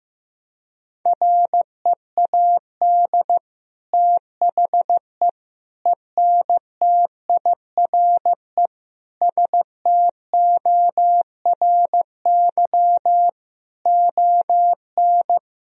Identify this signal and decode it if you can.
Morse code, heard as a keyed tone, about 15 wpm, 700 Hz. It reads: READ THE ENTIRE STORY ON
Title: Sentences from Top 1000 Words - Review Part 4 - 15wpm